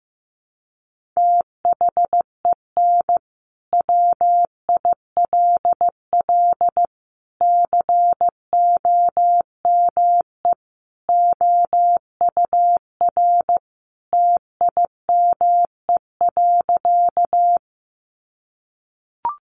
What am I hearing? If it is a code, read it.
Morse code, heard as a keyed tone, about 15 wpm, 700 Hz. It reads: THEN WILL COME OUR TIME.
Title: Sentences from Top 200 Words 15wpm